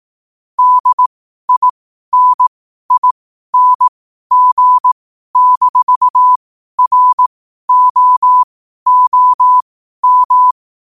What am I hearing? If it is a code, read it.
Morse code, heard as a keyed tone, about 18 wpm, spaced as for 12 wpm, 1000 Hz. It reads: DINING-ROOM
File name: qotdfiles / 12wpm